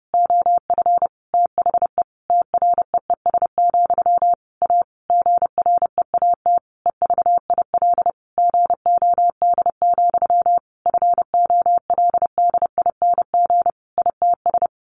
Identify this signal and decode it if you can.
Morse code, heard as a keyed tone, about 30 wpm, 700 Hz. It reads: OF THE TREES, A GREAT EVIL GOD, FOLDING ITS
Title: A Sound Of Thunder 30wpm